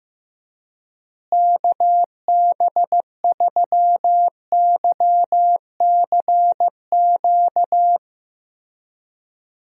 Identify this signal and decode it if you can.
Morse code, heard as a keyed tone, about 16 wpm, 700 Hz. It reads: KB3YCQ
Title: USA Call Signs - 6 Characters - Repeated 3x - 15wpm